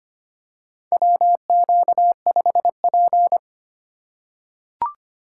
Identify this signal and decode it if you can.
Morse code, heard as a keyed tone, about 25 wpm, 700 Hz. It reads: WQ5P
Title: USA Call Signs - 4 Characters - 3x Character Spacing - 25wpm